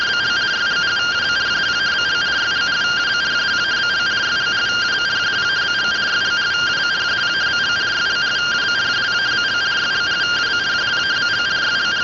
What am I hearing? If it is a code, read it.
Two-tone teleprinter signal, about 170 Hz shift.